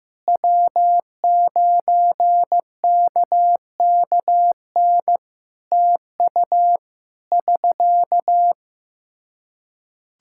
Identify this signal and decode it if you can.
Morse code, heard as a keyed tone, about 15 wpm, 700 Hz. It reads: W9KKN TU <SK>
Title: Practice QSO - Exchange 4 - Ending - 15wpm